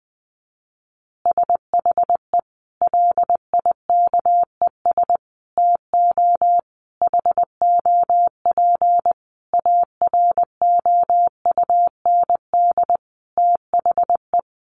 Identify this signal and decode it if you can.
Morse code, heard as a keyed tone, about 20 wpm, 700 Hz. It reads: SHE LIKES TO HOP AROUND THE
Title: Easy Phrases 20wpm